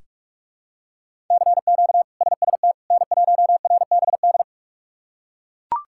Morse code, heard as a keyed tone, about 43 wpm, 700 Hz. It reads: CQ SST N1RBD
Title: K1USN SST QSO Practice - 45wpm